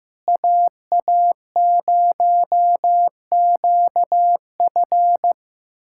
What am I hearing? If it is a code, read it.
Morse code, heard as a keyed tone, about 15 wpm, 700 Hz. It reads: AA0QF
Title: USA Callsign DE Callsign - 5 Characters - 15wpm